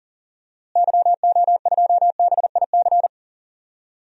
Morse code, heard as a keyed tone, about 40 wpm, 700 Hz. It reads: YO2BIC
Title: DXCC-100 Call Signs - 6 Characters - Repeated 3x - 40wpm